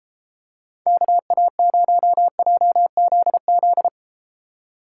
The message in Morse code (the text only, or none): KA0JZZ